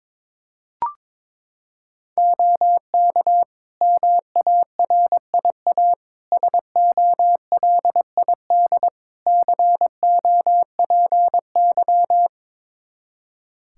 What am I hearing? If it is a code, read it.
Morse code, heard as a keyed tone, about 22 wpm, 700 Hz. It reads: OK MARIA SOLID COPY
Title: Practice QSO - Exchange 3 - More Info - 22wpm